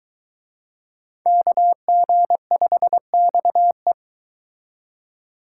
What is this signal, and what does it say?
Morse code, heard as a keyed tone, about 22 wpm, 700 Hz. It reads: KG5XE